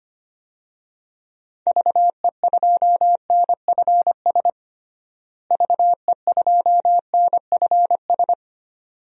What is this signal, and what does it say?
Morse code, heard as a keyed tone, about 25 wpm, 700 Hz. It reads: VE2NFS VE2NFS
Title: DXCC-100 Call Signs - 6 Characters - Repeated 2x - 25wpm